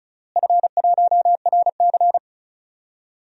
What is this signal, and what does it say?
Morse code, heard as a keyed tone, about 36 wpm, 700 Hz. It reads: F1RC
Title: DXCC-100 Call Signs - 4 Characters - Repeated 3x - 35wpm